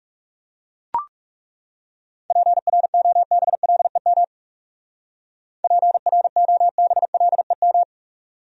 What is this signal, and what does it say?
Morse code, heard as a keyed tone, about 35 wpm, 700 Hz. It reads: PROBLEM PROBLEM
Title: Speed-Racing Top 500 Words 30wpm